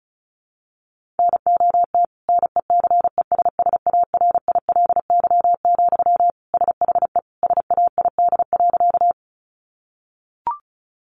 Morse code, heard as a keyed tone, about 34 wpm, 700 Hz. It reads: NOT NECESSARILY, SHE SAID.
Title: Sentences from Top 1400 Words 35wpm